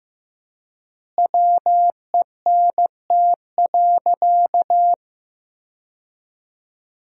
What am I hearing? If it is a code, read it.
Morse code, heard as a keyed tone, about 14 wpm, 700 Hz. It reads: WENT.
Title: 4x Word-Spacing - Sentences from Top 200 Words - 15wpm